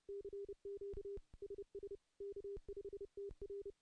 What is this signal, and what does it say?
Morse code, heard as a keyed tone, about 30 wpm, 400 Hz. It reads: CQ SS K5TR
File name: contests / CW-SS-k5tr- -0400